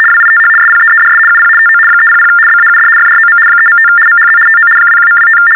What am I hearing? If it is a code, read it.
RTTY, about 75 baud. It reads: WITH SPEEDS UP TO 300 BAUD FROM 3.5 MHZ TO 21.25
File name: teletype-RTTY-75